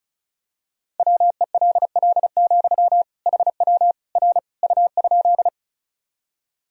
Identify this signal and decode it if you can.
Morse code, heard as a keyed tone, about 36 wpm, 700 Hz. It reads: WELL, HW R U?
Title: Sentences from Top 200 Words and Abbreviations 35wpm